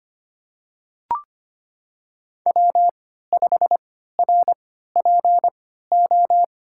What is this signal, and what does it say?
Morse code, heard as a keyed tone, about 26 wpm, spaced as for 14 wpm, 700 Hz. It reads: W5RPO
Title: USA Call Signs - 5 Characters - 3x Character Spacing - 25wpm